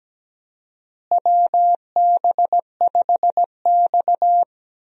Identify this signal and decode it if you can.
Morse code, heard as a keyed tone, about 17 wpm, 700 Hz. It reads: WB5X
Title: USA Call Signs - 4 Characters - Repeated 3x - 17wpm